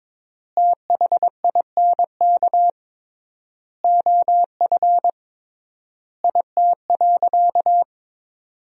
Morse code, heard as a keyed tone, about 22 wpm, 700 Hz. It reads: THINK OF IT.
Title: 3x Word-Spacing - Sentences from Top 200 Words - 22wpm